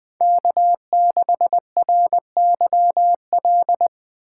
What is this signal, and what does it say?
Morse code, heard as a keyed tone, about 20 wpm, 700 Hz. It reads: K6RYL
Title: USA Call Signs - 5 Characters - 8x Character Spacing - 20wpm